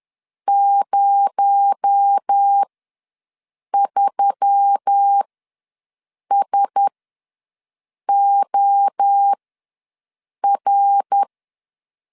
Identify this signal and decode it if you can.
Morse code, heard as a keyed tone, about 10 wpm, spaced as for 6 wpm, 800 Hz. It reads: 03SOR
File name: LEZIONI / 6 PKX0 / 6 PKX0 ws 10wpm 4220